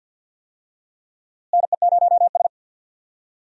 Morse code, heard as a keyed tone, about 52 wpm, 700 Hz. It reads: NE0S
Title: USA Call Signs - 4 Characters - Repeated 3x - 50wpm